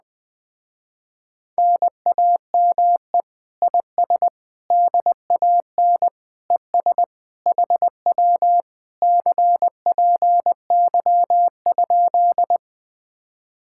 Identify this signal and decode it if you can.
Morse code, heard as a keyed tone, about 20 wpm, 700 Hz. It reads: NAME IS DAN ES HW CPY?
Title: Speed-Racing Common QSO Phrases Encore - 17wpm